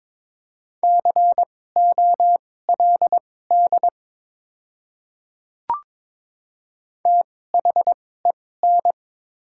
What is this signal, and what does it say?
Morse code, heard as a keyed tone, about 22 wpm, spaced as for 16 wpm, 690 Hz. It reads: COLD THEN
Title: 2x Character Spacing - Top 500 Words - 22wpm